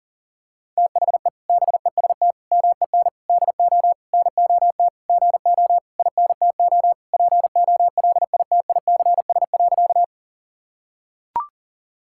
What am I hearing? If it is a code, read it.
Morse code, heard as a keyed tone, about 37 wpm, 700 Hz. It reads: THE BEST MEN DO NOT GO INTO POLITICS.
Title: Sentences from Top 1000 Words - Review Part 6 - 40wpm